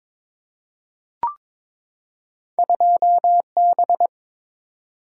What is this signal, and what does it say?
Morse code, heard as a keyed tone, about 22 wpm, 700 Hz. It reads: 2B
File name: practice / 22wpm / Two Letter-Number Combinations - 22wpm